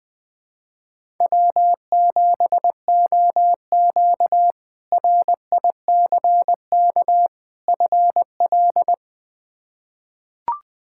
Morse code, heard as a keyed tone, about 21 wpm, 700 Hz. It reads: W7OQ RICK FL
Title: K1USN SST QSO Practice - 20wpm